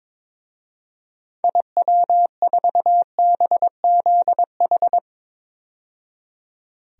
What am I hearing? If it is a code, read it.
Morse code, heard as a keyed tone, about 22 wpm, 700 Hz. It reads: IW4BZH
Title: DXCC-100 Call Signs - 6 Characters - 22wpm